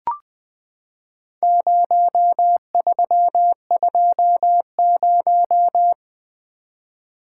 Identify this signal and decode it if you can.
Morse code, heard as a keyed tone, about 20 wpm, 700 Hz. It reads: 0320
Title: Four Letter-Number Combinations - 20wpm